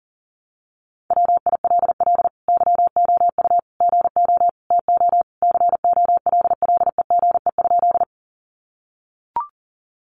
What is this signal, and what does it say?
Morse code, heard as a keyed tone, about 40 wpm, 700 Hz. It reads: WILL YOU GO TO COLLEGE?
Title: Sentences from Top 1300 Words 40wpm